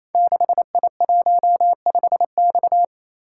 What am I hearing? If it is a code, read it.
Morse code, heard as a keyed tone, about 28 wpm, 700 Hz. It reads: 6I15X
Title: Code Groups - 5 Letter:Numbers - 28wpm